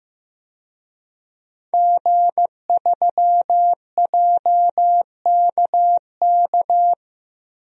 Morse code, heard as a keyed tone, about 15 wpm, 700 Hz. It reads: G3JKK
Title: DXCC-100 Call Signs - 5 Characters - 15wpm